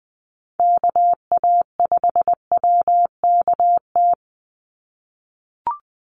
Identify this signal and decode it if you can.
Morse code, heard as a keyed tone, about 19 wpm, 700 Hz. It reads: KA5WKT
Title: USA Call Signs 20wpm